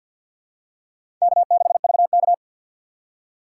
Morse code, heard as a keyed tone, about 51 wpm, 700 Hz. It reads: K6VK
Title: USA Call Signs - 6x Character Spacing - 50wpm